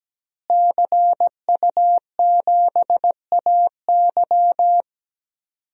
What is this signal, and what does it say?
Morse code, heard as a keyed tone, about 17 wpm, 700 Hz. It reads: CU7AY
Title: DXCC-100 Call Signs - 5 Characters - Repeated 3x - 17wpm